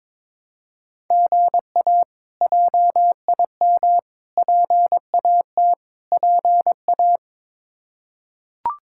Morse code, heard as a keyed tone, about 22 wpm, 700 Hz. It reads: GA JIM PAT PA
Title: K1USN SST QSO Practice - 22wpm